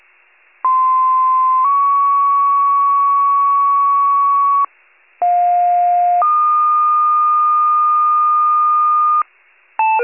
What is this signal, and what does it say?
Two-tone page, heard as a one-second tone, then a three-second tone, about 1040 then 1120 Hz, about 710 then 1220 Hz. It is heard from 0.6 to 9.2 s.
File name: wcalert